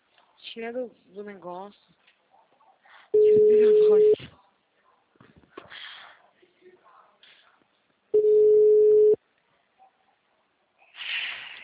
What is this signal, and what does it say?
Ringback tone (Europe, 425 Hz), with speech.
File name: out- -2013- - - .33294